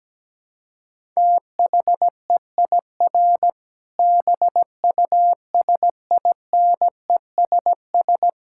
Morse code, heard as a keyed tone, about 17 wpm, 700 Hz. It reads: THEIR BUSINESS
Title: Sentences from Top 1000 Words - Review Part 5 - 17wpm